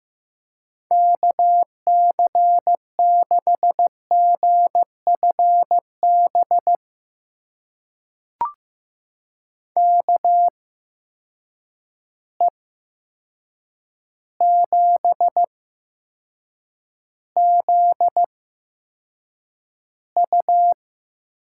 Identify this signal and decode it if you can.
Morse code, heard as a keyed tone, about 15 wpm, spaced as for 7 wpm, 700 Hz. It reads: KC6GFB KE7ZU
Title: USA Call Signs - 6 Characters - 8x Character Spacing - 15wpm